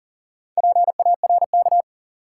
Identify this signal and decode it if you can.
Morse code, heard as a keyed tone, about 40 wpm, 700 Hz. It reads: PARK
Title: Sentences from Top 1300 Words - New Word Drill Alt - 40wpm